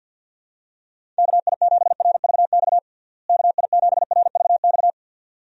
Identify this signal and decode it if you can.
Morse code, heard as a keyed tone, about 51 wpm, 700 Hz. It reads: KI7RVX KI7RVX
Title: USA Call Signs - 6 Characters - Repeated 2x - 50wpm